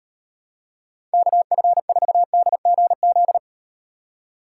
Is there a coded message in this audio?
Yes: Morse code from 1.1 to 3.4 s, heard as a keyed tone, about 38 wpm, 700 Hz. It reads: KF4DGZ